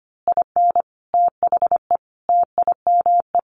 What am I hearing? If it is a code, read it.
Morse code, heard as a keyed tone, about 25 wpm, 700 Hz. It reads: IN THE TIME
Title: A Sound Of Thunder 25wpm